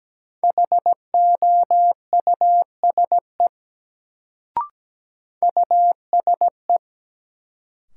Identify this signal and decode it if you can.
Morse code, heard as a keyed tone, about 17 wpm, 700 Hz. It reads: HOUSE USE
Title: Sets of 2 Words - New Word Drill - 17wpm